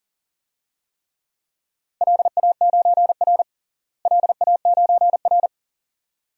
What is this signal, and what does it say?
Morse code, heard as a keyed tone, about 37 wpm, 700 Hz. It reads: LA9R LA9R
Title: DXCC-100 Call Signs - 4 Characters - Repeated 2x - 40wpm